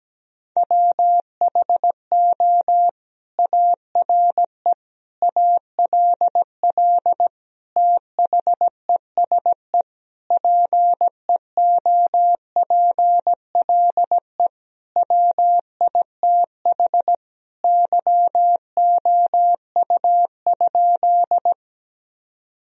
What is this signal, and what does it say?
Morse code, heard as a keyed tone, about 17 wpm, 700 Hz. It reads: WHO ARE ALL THESE PEOPLE WITH YOU?
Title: Sentences from Top 1000 Words - Review Part 1 - 17wpm